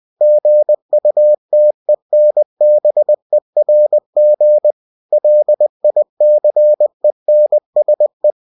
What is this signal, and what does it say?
Morse code, heard as a keyed tone, about 20 wpm, 600 Hz. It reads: GUTENBERG LICENSE
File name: winnie-20wpm